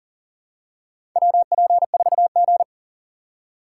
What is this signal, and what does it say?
Morse code, heard as a keyed tone, about 40 wpm, 700 Hz. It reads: WP4G
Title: USA Call Signs - 4 Characters - 4x Character Spacing - 40wpm